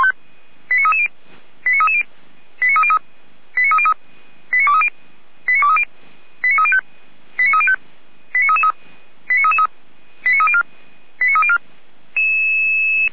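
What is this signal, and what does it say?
Five-tone selective call (ZVEI), heard as quick runs of five tones, tones of about 70 ms. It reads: 89330 89330 89303 89303 89230 89230 89305 89305 89333 89333 89304 89304
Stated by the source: ZVEI five-tone selective call 89319 received on an SDR, from the feuerwehr-orlamuende audio server.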